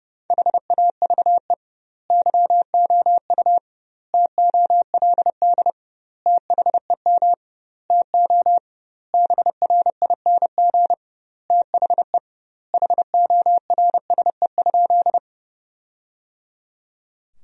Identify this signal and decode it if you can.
Morse code, heard as a keyed tone, about 32 wpm, 700 Hz. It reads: HAVE YOU TOLD THEM TO BRING THE HORSE?
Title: 2x Word-Spacing - Sentences from Top 500 Words - 30wpm